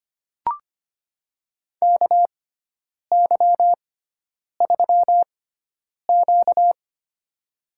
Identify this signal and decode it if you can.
Morse code, heard as a keyed tone, about 25 wpm, spaced as for 9 wpm, 700 Hz. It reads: KY3Q
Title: USA Call Signs - 4 Characters - 6x Character Spacing - 25wpm